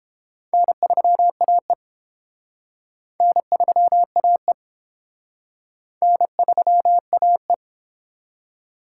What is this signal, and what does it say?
Morse code, heard as a keyed tone, about 29 wpm, 700 Hz. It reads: N3AE N3AE N3AE